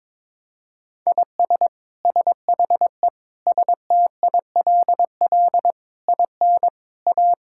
Morse code, heard as a keyed tone, about 22 wpm, 700 Hz. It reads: IS SHE STILL IN A
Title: Sentences from Top 1000 Words - Review Part 5 - 22wpm